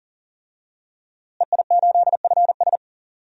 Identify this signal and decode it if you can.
Morse code, heard as a keyed tone, about 40 wpm, 700 Hz. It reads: EI8FS
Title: DXCC-100 Call Signs - 5 Characters - Repeated 3x - 40wpm